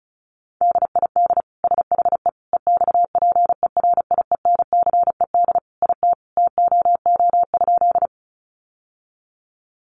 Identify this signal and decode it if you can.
Morse code, heard as a keyed tone, about 34 wpm, 700 Hz. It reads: DID SHE EXPERIENCED IT TOO?
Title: Sentences from Top 1100 Words 35wpm